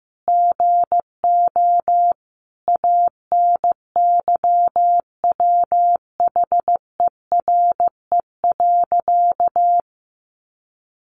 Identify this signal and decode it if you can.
Morse code, heard as a keyed tone, about 14 wpm, 700 Hz. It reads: GO ANYWHERE.
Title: Sentences from Top 1400 Words 15wpm